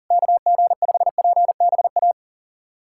Morse code, heard as a keyed tone, about 37 wpm, 700 Hz. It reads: KG5PBA
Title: USA Callsign DE Callsign - 6 Characters - 40wpm